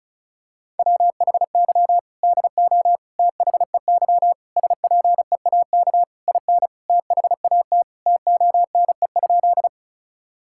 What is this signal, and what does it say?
Morse code, heard as a keyed tone, about 34 wpm, 700 Hz. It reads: WHY DO THEY SPEAK IN THAT TONE?
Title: Sentences from Top 1600 Words 35wpm